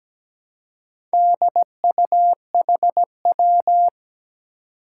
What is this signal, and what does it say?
Morse code, heard as a keyed tone, about 17 wpm, 700 Hz. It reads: DUHW
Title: Four Challenging Letter-Number Combinations 17wpm